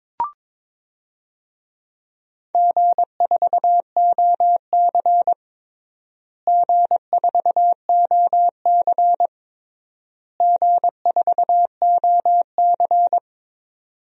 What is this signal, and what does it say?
Morse code, heard as a keyed tone, about 22 wpm, 700 Hz. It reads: G4OC G4OC G4OC
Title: DXCC-100 Call Signs - 4 Characters - Repeated 3x - 22wpm